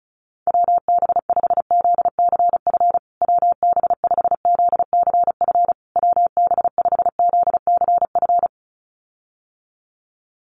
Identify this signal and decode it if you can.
Morse code, heard as a keyed tone, about 34 wpm, 700 Hz. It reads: WB5ZCF WB5ZCF WB5ZCF
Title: USA Call Signs Repeated 3x 35wpm